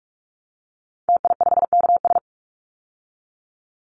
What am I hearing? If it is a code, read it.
Morse code, heard as a keyed tone, about 44 wpm, 700 Hz. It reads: TI5KS
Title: Call Signs Repeated 2x - 45wpm